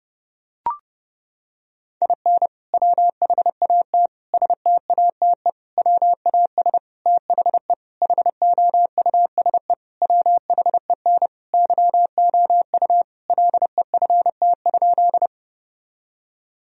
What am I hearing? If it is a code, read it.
Morse code, heard as a keyed tone, about 30 wpm, 700 Hz. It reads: IN WHAT STATE WAS THE HOUSE WHEN YOU LEFT?
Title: Sentences from Top 1000 Words - Review Part 2 - 30wpm